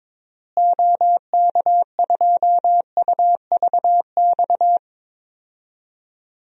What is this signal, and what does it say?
Morse code, heard as a keyed tone, about 22 wpm, 700 Hz. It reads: OK2UVX
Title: DXCC-100 Call Signs - 6 Characters - Repeated 3x - 22wpm